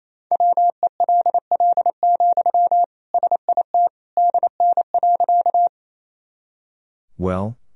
Morse code, heard as a keyed tone, about 28 wpm, 700 Hz. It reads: WELL, SIT DN.
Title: Sentences from Top 500 Words and Abbreviations 28wpm